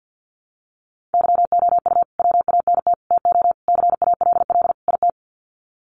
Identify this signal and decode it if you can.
Morse code, heard as a keyed tone, about 52 wpm, 700 Hz. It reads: YOU WANT TO CALL IT
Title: Sets of 5 Words Encore2 50wpm